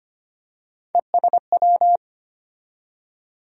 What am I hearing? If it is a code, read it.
Morse code, heard as a keyed tone, about 25 wpm, 700 Hz. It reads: ESW
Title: USA Call Sign Postfixes - 3 Letters 25wpm